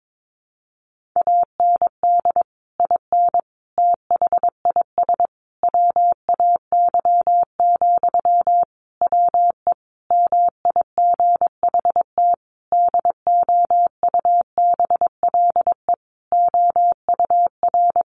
Morse code, heard as a keyed tone, about 22 wpm, 700 Hz. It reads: AND IN THIS WAY, WE MIGHT DOUBLE OUR
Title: Sentences from Top 1400 Words 22wpm